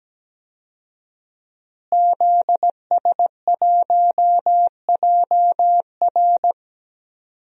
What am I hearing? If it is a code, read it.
Morse code, heard as a keyed tone, about 17 wpm, 700 Hz. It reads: ZS1JR
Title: DXCC-100 Call Signs - 5 Characters - Repeated 3x - 17wpm